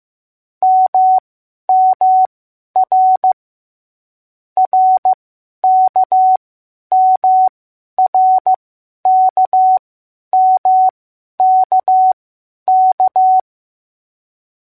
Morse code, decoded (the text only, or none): MMR RKMRKMKK